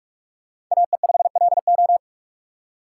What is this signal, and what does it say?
Morse code, heard as a keyed tone, about 44 wpm, 690 Hz. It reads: AE5LO
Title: USA Call Signs - 5 Characters - 2x Character Spacing - 45wpm